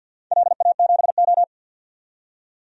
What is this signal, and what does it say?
Morse code, heard as a keyed tone, about 53 wpm, 700 Hz. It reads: LA7O
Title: DXCC-100 Call Signs - 4 Characters - 50wpm